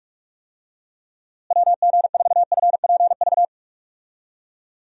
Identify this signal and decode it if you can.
Morse code, heard as a keyed tone, about 46 wpm, 700 Hz. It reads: WG4FPV
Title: USA Call Signs - 6 Characters - Repeated 2x - 45wpm